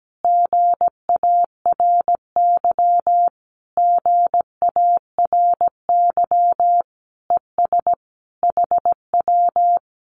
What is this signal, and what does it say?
Morse code, heard as a keyed tone, about 17 wpm, 700 Hz. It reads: GARY GARY ES HW
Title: Common QSO Phrases Encore 17wpm